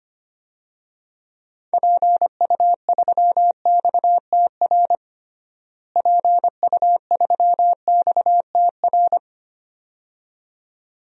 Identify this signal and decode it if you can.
Morse code, heard as a keyed tone, about 25 wpm, 700 Hz. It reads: PU3XTR PU3XTR
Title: DXCC-100 Call Signs - 6 Characters - Repeated 2x - 25wpm